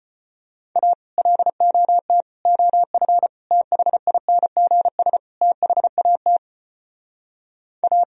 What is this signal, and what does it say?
Morse code, heard as a keyed tone, about 30 wpm, 700 Hz. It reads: A LOT OF THINGS THAT A
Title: Speed-Racing - Sets of 5 Words 25wpm